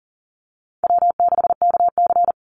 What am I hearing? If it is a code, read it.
Morse code, heard as a keyed tone, about 40 wpm, 700 Hz. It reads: W6KC
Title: Call Signs - 40wpm